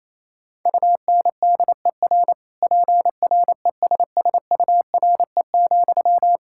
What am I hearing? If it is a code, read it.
Morse code, heard as a keyed tone, about 28 wpm, 700 Hz. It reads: UNDER PRESSURE,
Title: Sentences from Top 1000 Words - Review Part 6 - 28wpm